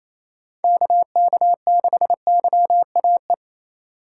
Morse code, heard as a keyed tone, about 36 wpm, 700 Hz. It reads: KK6YAE